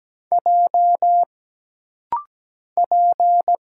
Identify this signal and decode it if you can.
Morse code, heard as a keyed tone, about 17 wpm, 700 Hz. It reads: J P
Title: Single Letter-Number - Rapid-Fire 17wpm